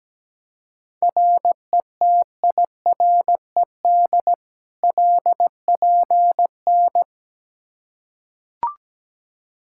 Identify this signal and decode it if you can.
Morse code, heard as a keyed tone, about 17 wpm, 700 Hz. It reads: RETIRED LPN
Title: Practice QSO - Exchange 3 - More Info - 17wpm